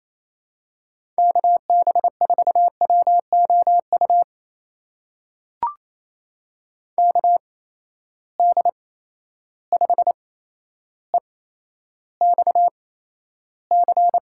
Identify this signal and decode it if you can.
Morse code, heard as a keyed tone, about 28 wpm, spaced as for 12 wpm, 700 Hz. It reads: KB4WOU KD5EXC
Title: USA Call Signs - 6 Characters - 8x Character Spacing - 28wpm